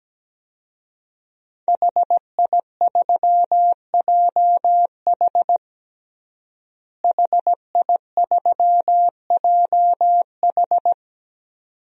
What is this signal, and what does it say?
Morse code, heard as a keyed tone, about 17 wpm, 700 Hz. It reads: HI3JH HI3JH
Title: DXCC-100 Call Signs - 5 Characters - Repeated 2x - 17wpm